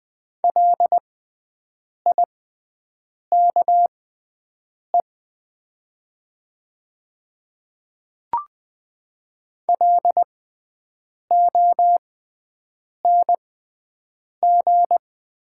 Morse code, heard as a keyed tone, about 20 wpm, spaced as for 7 wpm, 700 Hz. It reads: LIKE LONG
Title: 6x Character Spacing - Top 200 Words - 20wpm